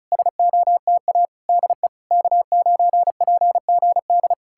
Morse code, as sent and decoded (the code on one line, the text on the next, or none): ... --- - .- / -.. / -.- ----. .--. --. -..
SOTA DE K9PGD